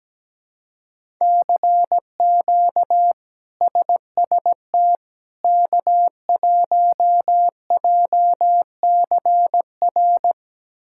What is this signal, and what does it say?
Morse code, heard as a keyed tone, about 17 wpm, 700 Hz. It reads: CQ SST K1JCR
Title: K1USN SST QSO Practice - 17wpm